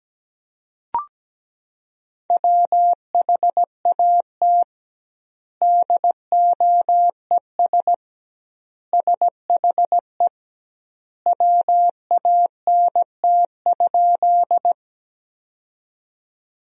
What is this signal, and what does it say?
Morse code, heard as a keyed tone, about 17 wpm, 700 Hz. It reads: WHAT DOES SHE WANT?
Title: 2x Word-Spacing - Sentences from Top 200 Words - 17wpm